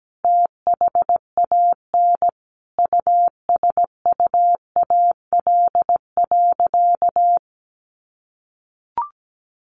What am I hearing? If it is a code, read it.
Morse code, heard as a keyed tone, about 17 wpm, 700 Hz. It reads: THAN USUAL.
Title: Sentences from Top 1400 Words 17wpm